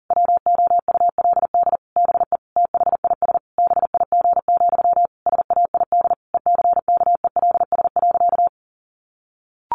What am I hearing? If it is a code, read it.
Morse code, heard as a keyed tone, about 37 wpm, 700 Hz. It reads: WOULD BE THIS BIG, SAID ECKELS.
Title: A Sound Of Thunder 40wpm